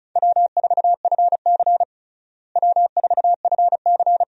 Morse code, heard as a keyed tone, about 36 wpm, 700 Hz. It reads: W4FC W4FC
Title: USA Call Signs - 4 Characters - Repeated 2x - 35wpm